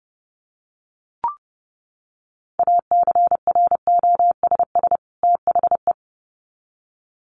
Morse code, heard as a keyed tone, about 30 wpm, 700 Hz. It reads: ACROSS THE
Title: Sets of 2 Words 30wpm